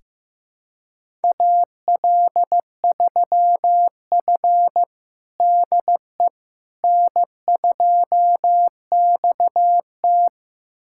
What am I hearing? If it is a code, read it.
Morse code, heard as a keyed tone, about 15 wpm, 700 Hz. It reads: AL3F DE N2XT
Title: USA Callsign DE Callsign - 4 Characters - 15wpm